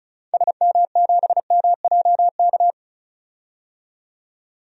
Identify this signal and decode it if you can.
Morse code, heard as a keyed tone, about 36 wpm, 700 Hz. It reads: SM7MJK
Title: DXCC-100 Call Signs - 6 Characters - 35wpm